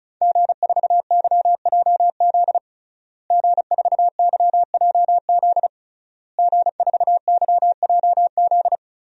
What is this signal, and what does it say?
Morse code, heard as a keyed tone, about 34 wpm, 700 Hz. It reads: G4YJZ G4YJZ G4YJZ
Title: DXCC-100 Call Signs - 5 Characters - Repeated 3x - 35wpm